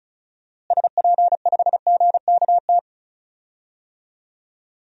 Morse code, heard as a keyed tone, about 36 wpm, 700 Hz. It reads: SP5GKT
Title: DXCC-100 Call Signs - 6 Characters - Repeated 3x - 35wpm